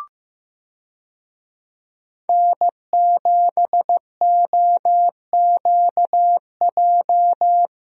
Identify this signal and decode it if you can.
Morse code, heard as a keyed tone, about 15 wpm, 700 Hz. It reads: N7OQJ
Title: USA Call Signs - 5 Characters - Repeated 3x - 15wpm